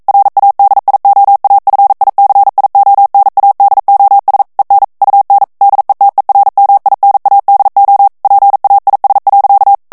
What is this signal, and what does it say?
Morse code, heard as a keyed tone, about 42 wpm, 800 Hz. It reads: RADIOAFICIONADOS EN UN DETERMINADO PAIS.
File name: morse mp3 / ESPAGNOL TEXTE EN CLAIR 1 VITESSE 40 MOTS MINUTES